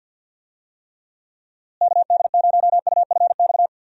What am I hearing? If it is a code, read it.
Morse code, heard as a keyed tone, about 46 wpm, 700 Hz. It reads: KD0UFX
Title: USA Call Signs - 6 Characters - Repeated 2x - 50wpm